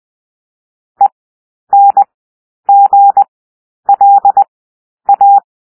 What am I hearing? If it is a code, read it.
Morse code, heard as a keyed tone, about 20 wpm, spaced as for 10 wpm, 800 Hz. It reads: ENGLA
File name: cwmp3 / W1 SOUND WORDS 20 wpm